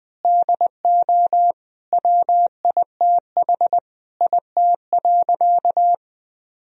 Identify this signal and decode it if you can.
Morse code, heard as a keyed tone, about 21 wpm, 700 Hz. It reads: DO WITH IT.
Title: Sentences from Top 1000 Words - Review Part 2 - 20wpm